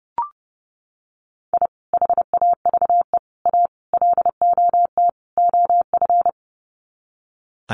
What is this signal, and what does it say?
Morse code, heard as a keyed tone, about 30 wpm, 700 Hz. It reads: I HAVE A LOT OF
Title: Sets of 5 Words 30wpm